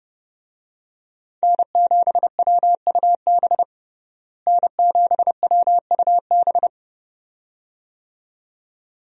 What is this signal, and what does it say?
Morse code, heard as a keyed tone, about 32 wpm, 700 Hz. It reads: N7WUB N7WUB
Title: USA Call Signs - 5 Characters - Repeated 2x - 30wpm